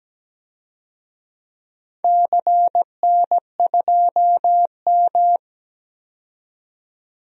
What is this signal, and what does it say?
Morse code, heard as a keyed tone, about 17 wpm, 700 Hz. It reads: CN2M
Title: DXCC-100 Call Signs - 4 Characters - 17wpm